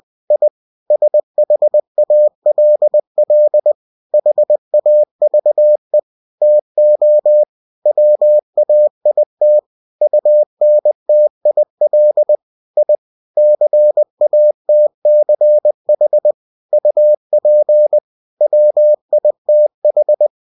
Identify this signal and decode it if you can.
Morse code, heard as a keyed tone, about 20 wpm, 600 Hz. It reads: I SHALL HAVE TO WAIT UNTIL I CATCH UP WITH
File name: winnie-20wpm